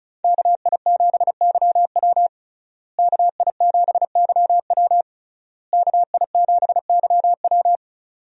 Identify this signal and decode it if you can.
Morse code, heard as a keyed tone, about 34 wpm, 700 Hz. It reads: KI7YW KI7YW KI7YW
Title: USA Call Signs - 5 Characters - Repeated 3x - 35wpm